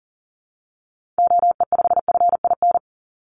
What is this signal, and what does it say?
Morse code, heard as a keyed tone, about 38 wpm, 700 Hz. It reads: OE5FIN
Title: Call Signs - 40wpm